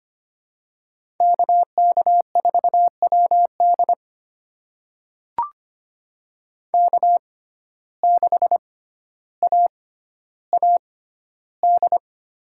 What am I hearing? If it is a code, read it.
Morse code, heard as a keyed tone, about 25 wpm, spaced as for 13 wpm, 700 Hz. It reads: KK4WD K6AAD
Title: USA Call Signs - 5 Characters - 6x Character Spacing - 25wpm